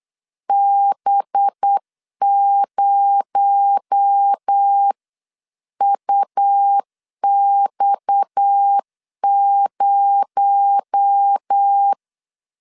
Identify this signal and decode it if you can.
Morse code, heard as a keyed tone, about 8 wpm, 800 Hz. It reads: B0UX0
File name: 8wpm